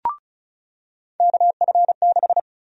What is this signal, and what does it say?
Morse code, heard as a keyed tone, about 34 wpm, 700 Hz. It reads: KF6
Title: USA Call Sign Prefixes - 35wpm